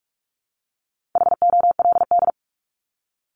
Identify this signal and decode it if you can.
Morse code, heard as a keyed tone, about 44 wpm, 700 Hz. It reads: HOLD
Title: ING Suffix - Top 100 Words with Base Words 45wpm